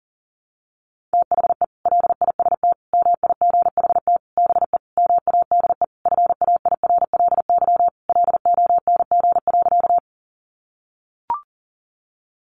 Morse code, heard as a keyed tone, about 37 wpm, 700 Hz. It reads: THE LIST MIGHT BE MADE FAIRLY LONG.
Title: Sentences from Top 1100 Words 40wpm